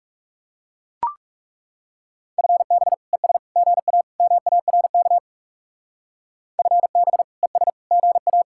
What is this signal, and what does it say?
Morse code, heard as a keyed tone, about 38 wpm, 700 Hz. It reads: FB ES GA MARK FB ES GA